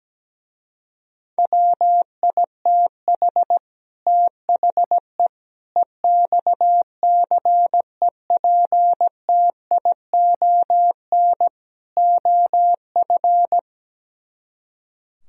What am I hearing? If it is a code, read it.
Morse code, heard as a keyed tone, about 17 wpm, 700 Hz. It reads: WITH THE EXCEPTION OF
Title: Speed-Racing - Sets of 4 Words 17wpm